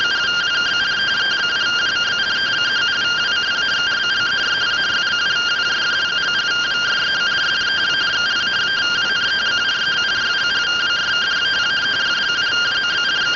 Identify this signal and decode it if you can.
Two-tone teleprinter signal, about 170 Hz shift.